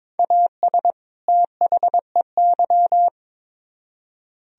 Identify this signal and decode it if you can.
Morse code, heard as a keyed tone, about 26 wpm, 700 Hz. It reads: AS THEY
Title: Speed-Racing - Sets of 2 Words 22wpm